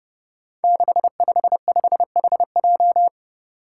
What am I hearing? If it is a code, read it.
Morse code, heard as a keyed tone, about 29 wpm, 700 Hz. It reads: 655HJ
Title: Five Challenging Letter-Number Combinations 30wpm